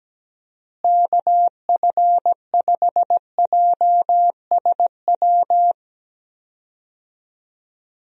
Morse code, heard as a keyed tone, about 17 wpm, 700 Hz. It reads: KF5JSW
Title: USA Call Signs - 6 Characters - Repeated 3x - 17wpm